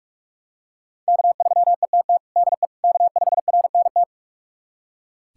Morse code, heard as a keyed tone, about 46 wpm, 700 Hz. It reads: K3ETT DE K5RNT
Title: USA Callsign DE Callsign - 5 Characters - 45wpm